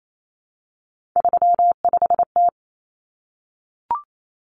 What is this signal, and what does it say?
Morse code, heard as a keyed tone, about 28 wpm, 700 Hz. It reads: 35T
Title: Common Cut Numbers - 3-Digit 28wpm